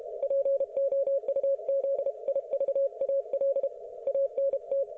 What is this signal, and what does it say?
Morse code, heard as a keyed tone, about 32 wpm, 550 Hz. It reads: POUZIVAL ANT